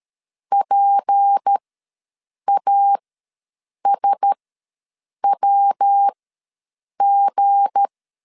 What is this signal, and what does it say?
Morse code, heard as a keyed tone, about 13 wpm, spaced as for 7 wpm, 800 Hz. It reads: PASWG
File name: LEZIONI / 9 2 ws 12wpm 3406